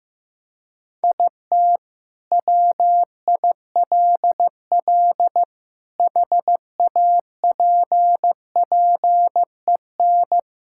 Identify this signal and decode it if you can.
Morse code, heard as a keyed tone, about 15 wpm, 700 Hz. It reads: IT WILL HAPPEN
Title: Sentences from Top 1000 Words - Review Part 5 - 15wpm